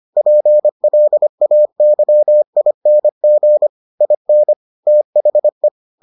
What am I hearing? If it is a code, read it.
Morse code, heard as a keyed tone, about 25 wpm, 600 Hz. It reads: PLAYING IN THE